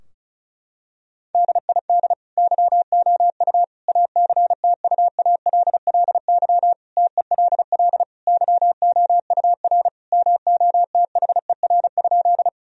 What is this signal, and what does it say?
Morse code, heard as a keyed tone, about 36 wpm, 700 Hz. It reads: DID YOU ACTUALLY TELL YOUR MOTHER?
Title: Sentences from Top 1000 Words - Review Part 4 - 35wpm